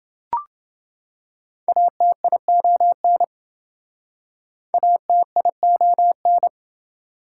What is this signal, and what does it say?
Morse code, heard as a keyed tone, about 28 wpm, 700 Hz. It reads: ATION ATION